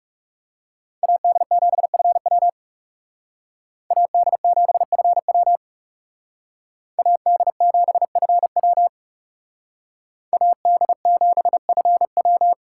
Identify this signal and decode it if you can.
Morse code, heard as a keyed tone, about 38 wpm, 700 Hz. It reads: AD7FW AD7FW AD7FW AD7FW